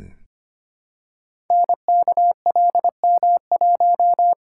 Morse code, heard as a keyed tone, about 25 wpm, 700 Hz. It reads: NKLM1
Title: Code Groups - 5 Letter:Numbers - 25wpm